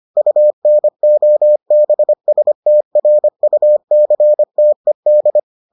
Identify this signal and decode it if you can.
Morse code, heard as a keyed tone, about 25 wpm, 600 Hz. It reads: UNOBSTRUCTED